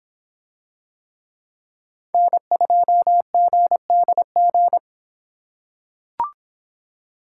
Morse code, heard as a keyed tone, about 24 wpm, 700 Hz. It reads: N2GDG